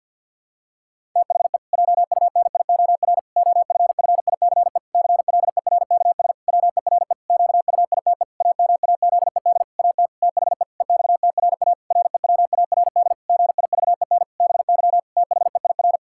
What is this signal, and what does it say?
Morse code, heard as a keyed tone, about 50 wpm, 700 Hz. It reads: THE JUNIOR OFFICE CLERKS WERE QUITE AMAZED AT THE EXTRA REWARD GIVEN BY THEIR
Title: Pangram Challenge v2 - 50wpm